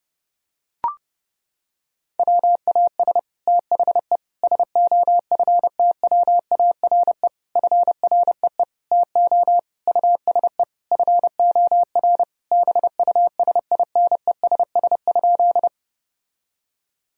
Morse code, heard as a keyed tone, about 30 wpm, 700 Hz. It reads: WAS THE SOFTWARE FREE TO USE FOR BUSINESS?
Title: Sentences from Top 1000 Words - Review Part 3 - 30wpm